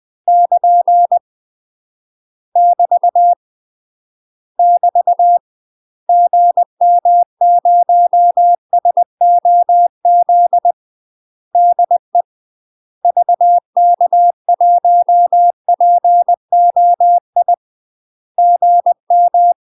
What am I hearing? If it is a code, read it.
Morse code, heard as a keyed tone, about 20 wpm, 700 Hz. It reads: <KN> <BT> <BT> GM0SOZ DE VK1POI GM
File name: UK MASTER QSOs TRAINING / 20WPM / UK MASTER QSOs 0042